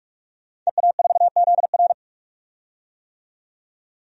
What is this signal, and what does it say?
Morse code, heard as a keyed tone, about 41 wpm, 700 Hz. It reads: EA4ZR
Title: DXCC-100 Call Signs - 5 Characters - 45wpm